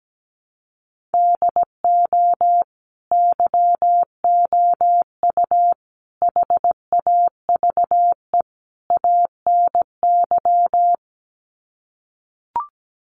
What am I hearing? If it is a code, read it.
Morse code, heard as a keyed tone, about 17 wpm, 700 Hz. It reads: DO YOU HAVE ANY
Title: Sets of 4 Words 17wpm